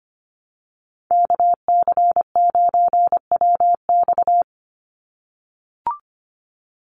Morse code, heard as a keyed tone, about 25 wpm, 700 Hz. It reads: KC9WX
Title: USA Call Signs 25wpm